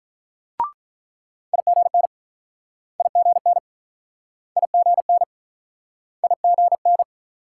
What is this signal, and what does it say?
Morse code, heard as a keyed tone, about 45 wpm, 700 Hz. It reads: IGN IGN IGN IGN